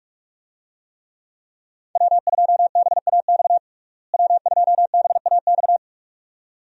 Morse code, heard as a keyed tone, about 47 wpm, 700 Hz. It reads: W2BAX W2BAX
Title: USA Call Signs - 5 Characters - Repeated 2x - 45wpm